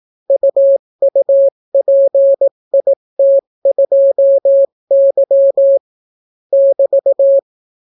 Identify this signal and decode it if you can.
Morse code, heard as a keyed tone, about 18 wpm, 550 Hz. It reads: UUPIT2Y <BT>